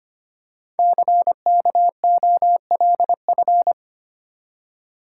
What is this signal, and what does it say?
Morse code, heard as a keyed tone, about 25 wpm, 700 Hz. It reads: CKOLF
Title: Code Groups - 5 Letters - 25wpm